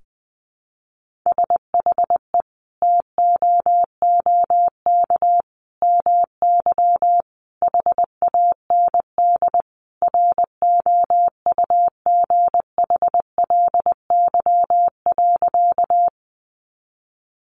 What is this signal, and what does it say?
Morse code, heard as a keyed tone, about 20 wpm, 700 Hz. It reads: SHE TOOK MY HAND ROUGHLY.
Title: Sentences from Top 1500 Words 20wpm